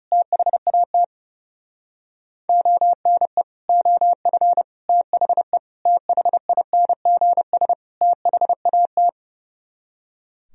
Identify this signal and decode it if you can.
Morse code, heard as a keyed tone, about 38 wpm, 700 Hz. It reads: THAT ONE OF THE THINGS THAT